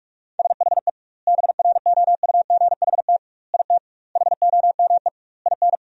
Morse code, heard as a keyed tone, about 46 wpm, 700 Hz. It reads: SHE BROUGHT IT HOME IN
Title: 2x Word-Spacing - Sentences from Top 500 Words - 45wpm